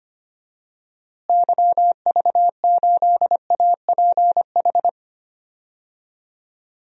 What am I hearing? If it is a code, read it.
Morse code, heard as a keyed tone, about 25 wpm, 700 Hz. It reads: YV8APH
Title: DXCC-100 Call Signs - 6 Characters - Repeated 3x - 25wpm